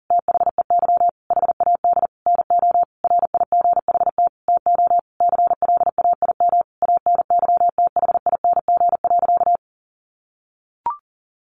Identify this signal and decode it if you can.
Morse code, heard as a keyed tone, about 40 wpm, 700 Hz. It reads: THEY HAD NO RIGHT TO CLAIM ANYTHING.
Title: Sentences from Top 1500 Words 40wpm